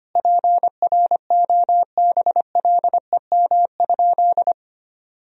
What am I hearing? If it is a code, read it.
Morse code, heard as a keyed tone, about 25 wpm, 700 Hz. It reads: PROBLEM?
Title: Sentences from Top 1000 Words - Review Part 3 - 25wpm